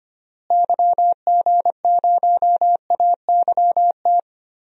Morse code, heard as a keyed tone, about 25 wpm, 700 Hz. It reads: YG0AYT
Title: DXCC-100 Call Signs - 6 Characters - Repeated 3x - 25wpm